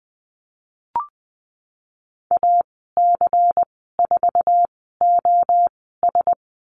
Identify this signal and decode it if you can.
Morse code, heard as a keyed tone, about 20 wpm, spaced as for 14 wpm, 700 Hz. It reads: AC4OS
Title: 128 Lesson 21g - Number 6 - Call Signs 20wpm